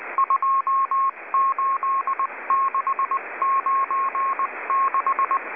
Morse code, heard as a keyed tone, about 19 wpm, 1080 Hz, with radio noise. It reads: 28696